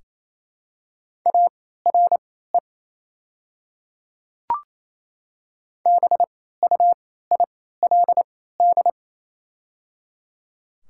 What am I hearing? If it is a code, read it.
Morse code, heard as a keyed tone, about 28 wpm, spaced as for 16 wpm, 700 Hz. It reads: ARE BUILD
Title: 3x Character Spacing - Top 200 Words - 28wpm